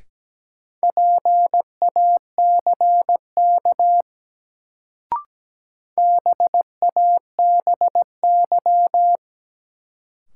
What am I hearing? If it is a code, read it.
Morse code, heard as a keyed tone, about 17 wpm, 700 Hz. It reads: PACK BABY
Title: Sentences from Top 1300 Words - New Word Drill - 17wpm